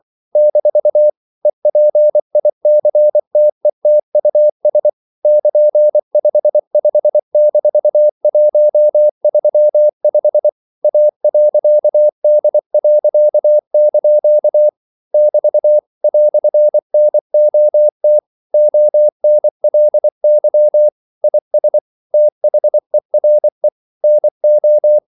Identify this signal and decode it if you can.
Morse code, heard as a keyed tone, about 24 wpm, 600 Hz. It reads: - EPICTETUS <KN>55-135 A.D.) <BT> "NOT ONLY IS THERE NO
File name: CW / Practice / EnglishQuotations0324